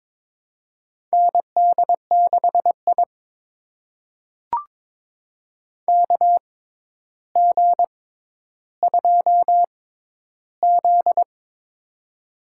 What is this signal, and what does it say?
Morse code, heard as a keyed tone, about 22 wpm, spaced as for 11 wpm, 700 Hz. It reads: ND6I KG2Z
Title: USA Call Signs - 4 Characters - 6x Character Spacing - 22wpm